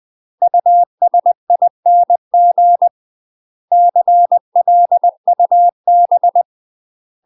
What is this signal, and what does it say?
Morse code, heard as a keyed tone, about 20 wpm, 700 Hz. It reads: USING CLUB
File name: UK MASTER QSOs TRAINING / 20WPM / UK MASTER QSOs 0049